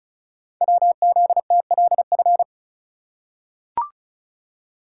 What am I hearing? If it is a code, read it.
Morse code, heard as a keyed tone, about 36 wpm, 700 Hz. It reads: WZTLF
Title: Code Groups - 5 Letters - 35wpm